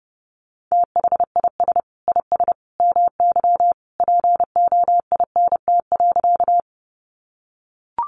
Morse code, heard as a keyed tone, about 30 wpm, 700 Hz. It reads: THIS IS MY POINT.
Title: Sentences from Top 200 Words 30wpm